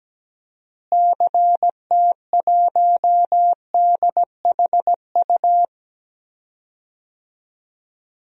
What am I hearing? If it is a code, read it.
Morse code, heard as a keyed tone, about 17 wpm, 700 Hz. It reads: CT1DHU
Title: DXCC-100 Call Signs - 6 Characters - Repeated 3x - 17wpm